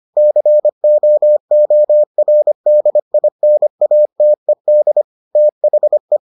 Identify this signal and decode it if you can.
Morse code, heard as a keyed tone, about 25 wpm, 600 Hz. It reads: COORDINATED THE